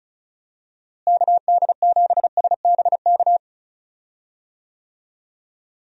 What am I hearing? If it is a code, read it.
Morse code, heard as a keyed tone, about 34 wpm, 700 Hz. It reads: KD7SBK
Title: USA Call Signs - 6 Characters - Repeated 3x - 35wpm